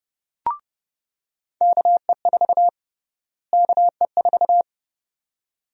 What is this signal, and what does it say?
Morse code, heard as a keyed tone, about 30 wpm, 700 Hz. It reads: KE4 KE4
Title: USA Call Sign Prefixes - Repeated 2x - 30wpm